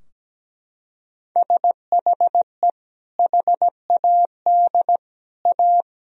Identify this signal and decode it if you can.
Morse code, heard as a keyed tone, about 17 wpm, 700 Hz. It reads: SHE HAD A
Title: Sentences from Top 1000 Words - Review Part 4 - 17wpm